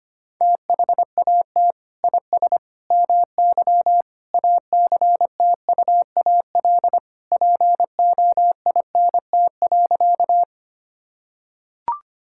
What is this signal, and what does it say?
Morse code, heard as a keyed tone, about 25 wpm, 700 Hz. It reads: THAT IS MY ACTUAL POINT.
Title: Sentences from Top 1000 Words - Review Part 5 - 25wpm